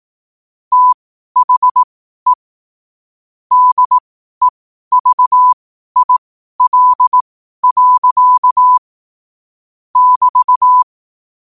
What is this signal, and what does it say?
Morse code, heard as a keyed tone, about 18 wpm, spaced as for 12 wpm, 1000 Hz. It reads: THE DEVIL. <BT>
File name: qotdfiles / 12wpm